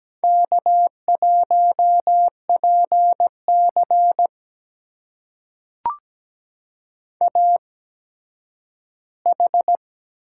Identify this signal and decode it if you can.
Morse code, heard as a keyed tone, about 17 wpm, spaced as for 10 wpm, 700 Hz. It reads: K1PC AH
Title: USA Call Signs - 8x Character Spacing - 17wpm